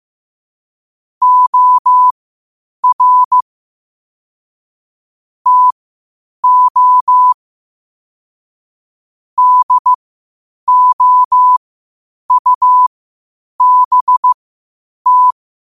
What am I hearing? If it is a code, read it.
Morse code, heard as a keyed tone, about 15 wpm, spaced as for 8 wpm, 1000 Hz. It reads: OR TO DOUBT
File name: qotdfiles / 8wpm